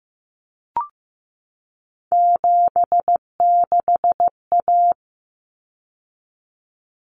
Morse code, heard as a keyed tone, about 15 wpm, 700 Hz. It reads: 76A
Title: Common Cut Numbers - 3-Digit 15wpm